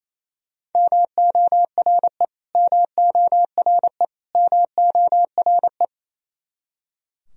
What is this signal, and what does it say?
Morse code, heard as a keyed tone, about 28 wpm, 700 Hz. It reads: MORE MORE MORE
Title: Top 100 Words Repeated 28wpm